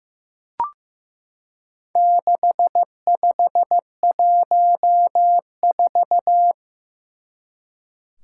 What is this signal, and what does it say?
Morse code, heard as a keyed tone, about 15 wpm, 700 Hz. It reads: 6514
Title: Four Challenging Letter-Number Combinations 15wpm